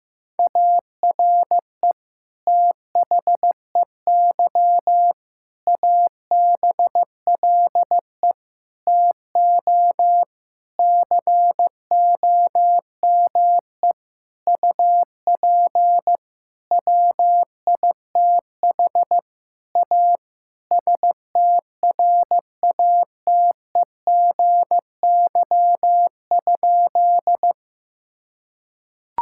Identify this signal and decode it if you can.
Morse code, heard as a keyed tone, about 16 wpm, 700 Hz. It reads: ARE THEY ABLE TO COME UP WITH A STRATEGY?
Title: Sentences from Top 1000 Words - Review Part 5 - 15wpm